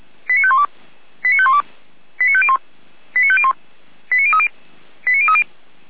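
Five-tone selective call (ZVEI), heard as quick runs of five tones, tones of about 70 ms. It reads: 89512 89512 89501 89501 89030 89030
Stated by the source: ZVEI five-tone call 89042 received by SDR.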